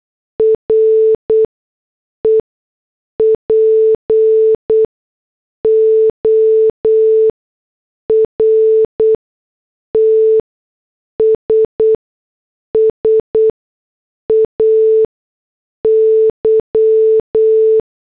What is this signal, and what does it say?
Morse code, heard as a keyed tone, about 8 wpm, spaced as for 6 wpm, 440 Hz. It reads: REPORTS SAY
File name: morse / headlines